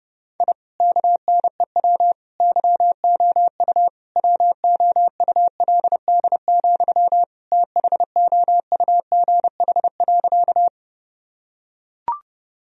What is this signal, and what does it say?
Morse code, heard as a keyed tone, about 30 wpm, 700 Hz. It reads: I KNEW YOU WOULD, THOUGH.
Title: Sentences from Top 1000 Words - Review Part 2 - 30wpm